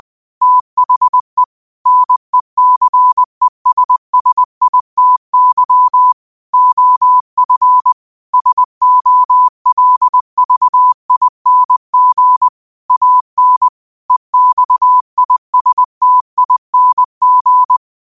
Morse code, heard as a keyed tone, about 20 wpm, 1000 Hz. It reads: THE NECESSITY OF SOLVING AN EXISTING
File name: qotdfiles / 20wpm